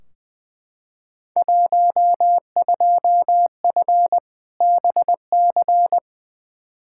Morse code, heard as a keyed tone, about 20 wpm, 700 Hz. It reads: 12F BC
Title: Field Day Exchanges Encore - 20wpm